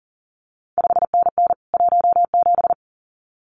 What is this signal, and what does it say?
Morse code, heard as a keyed tone, about 40 wpm, 700 Hz. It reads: 5NN 17
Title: CQ WW DX Contest Exchange Encore - 40wpm